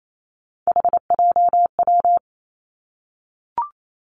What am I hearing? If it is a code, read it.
Morse code, heard as a keyed tone, about 28 wpm, 700 Hz. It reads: HJW
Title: Three Challenging Letter-Number Combinations 28wpm